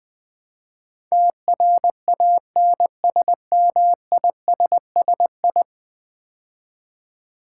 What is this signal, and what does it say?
Morse code, heard as a keyed tone, about 20 wpm, 700 Hz. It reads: TRANSMISSI
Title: Build A Word - 5 Letters At A Time - 12-Letter Words - 20wpm